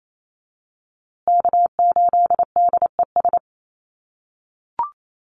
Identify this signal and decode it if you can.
Morse code, heard as a keyed tone, about 28 wpm, 700 Hz. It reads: K8DES
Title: USA Call Signs Repeated 3x 28wpm